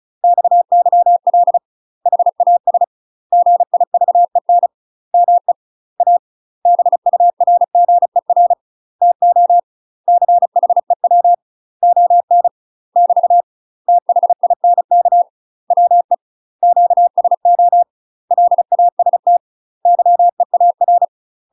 Morse code, heard as a keyed tone, about 36 wpm, 700 Hz. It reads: XYL HAS GIVEN ME A BURGER TO CHEW ON <BT> THINK WE QSO LAST YEAR
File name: UK MASTER QSOs TRAINING / 35WPM / UK MASTER QSOs 0007